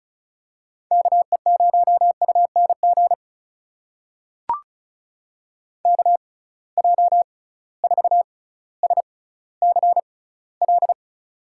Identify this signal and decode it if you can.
Morse code, heard as a keyed tone, about 36 wpm, spaced as for 18 wpm, 700 Hz. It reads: KE0UNG KJ4SCL
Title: USA Call Signs - 6 Characters - 6x Character Spacing - 35wpm